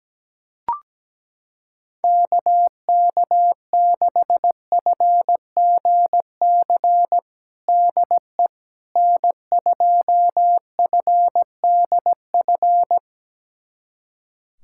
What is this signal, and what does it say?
Morse code, heard as a keyed tone, about 17 wpm, 700 Hz. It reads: KK6FGC DE N2FDF
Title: USA Callsign DE Callsign 17wpm